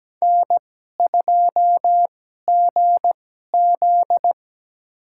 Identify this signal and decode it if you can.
Morse code, heard as a keyed tone, about 17 wpm, spaced as for 12 wpm, 700 Hz. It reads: N2GZ
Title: USA Call Signs - 2x Character Spacing - 17wpm